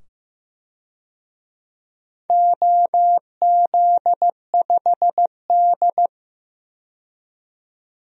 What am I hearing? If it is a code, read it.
Morse code, heard as a keyed tone, about 15 wpm, 700 Hz. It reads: OZ5D
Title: DXCC-100 Call Signs - 4 Characters - 15wpm